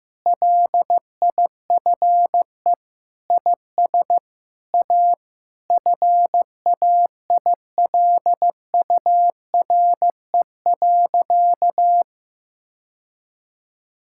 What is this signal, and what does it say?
Morse code, heard as a keyed tone, about 16 wpm, 700 Hz. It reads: LIFE IS A FAILURE.
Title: Sentences from Top 1000 Words - Review Part 6 - 15wpm